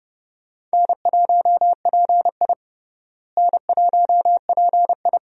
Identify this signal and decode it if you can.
Morse code, heard as a keyed tone, about 29 wpm, 700 Hz. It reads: N1PI N1PI
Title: USA Call Signs - 4 Characters - Repeated 2x - 30wpm